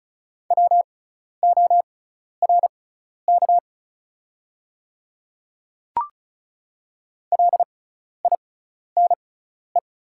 Morse code, heard as a keyed tone, about 36 wpm, spaced as for 12 wpm, 690 Hz. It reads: WORK LINE
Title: 6x Character Spacing - Top 200 Words - 35wpm